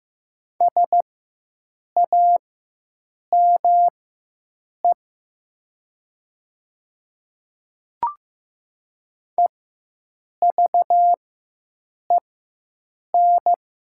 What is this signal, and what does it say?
Morse code, heard as a keyed tone, about 15 wpm, spaced as for 7 wpm, 700 Hz. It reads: SAME EVEN
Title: 4x Character Spacing - Top 200 Words - 15wpm